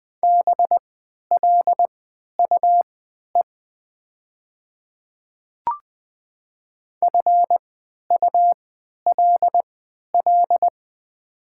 Morse code, heard as a keyed tone, about 20 wpm, spaced as for 11 wpm, 700 Hz. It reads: BLUE FULL
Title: 3x Character Spacing - Top 500 Words - 20wpm